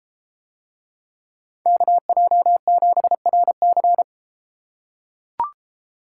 Morse code, heard as a keyed tone, about 31 wpm, 700 Hz. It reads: KJ7RC